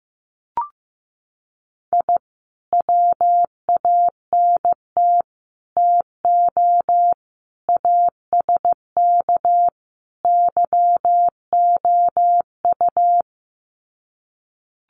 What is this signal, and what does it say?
Morse code, heard as a keyed tone, about 15 wpm, 700 Hz. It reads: I WANT TO ASK YOU
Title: Sets of 5 Words 15wpm